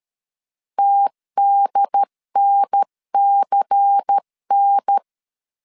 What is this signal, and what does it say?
Morse code, heard as a keyed tone, about 13 wpm, 800 Hz. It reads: TDNCN
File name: CW / DET / 1 6 1500C S C 12wpm 3533